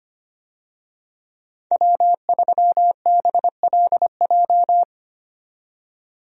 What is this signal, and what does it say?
Morse code, heard as a keyed tone, about 25 wpm, 690 Hz. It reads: W3BLJ
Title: USA Call Signs - 5 Characters - 25wpm